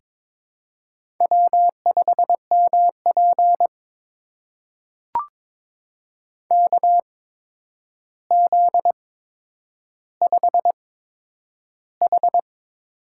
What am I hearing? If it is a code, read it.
Morse code, heard as a keyed tone, about 22 wpm, spaced as for 9 wpm, 700 Hz. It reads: W5MP KZ5H
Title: USA Call Signs - 4 Characters - 8x Character Spacing - 22wpm